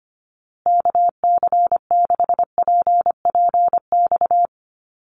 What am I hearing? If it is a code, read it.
Morse code, heard as a keyed tone, about 25 wpm, 700 Hz. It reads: KC6PPX
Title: USA Call Signs Repeated 3x 25wpm